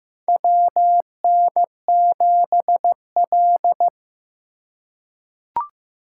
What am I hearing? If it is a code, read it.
Morse code, heard as a keyed tone, about 15 wpm, 700 Hz. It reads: WN7L
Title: USA Call Signs - 4 Characters - 4x Character Spacing - 15wpm